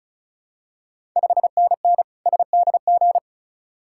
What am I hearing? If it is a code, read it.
Morse code, heard as a keyed tone, about 36 wpm, 690 Hz. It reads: 5NN SDG
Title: Ontario QSO Party Exchange 35wpm